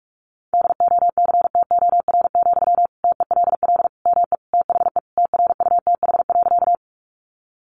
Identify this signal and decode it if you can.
Morse code, heard as a keyed tone, about 44 wpm, 700 Hz. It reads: DOCTOR, TELL ME THE TRUTH.
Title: Sentences from Top 1500 Words 45wpm